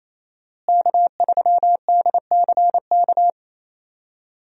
Morse code, heard as a keyed tone, about 27 wpm, 700 Hz. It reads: K3DCK
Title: USA Call Signs - 5 Characters - Repeated 2x - 28wpm